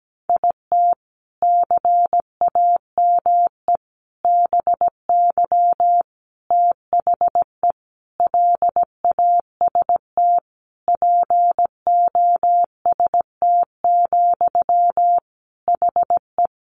Morse code, heard as a keyed tone, about 17 wpm, 700 Hz. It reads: IT CAME BY THE LAST POST, HE
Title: Sentences from Top 1000 Words 17wpm